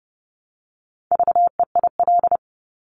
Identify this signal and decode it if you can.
Morse code, heard as a keyed tone, about 30 wpm, 690 Hz. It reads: VEIL
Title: Multi-letter Phonemes Repeated 3x 30wpm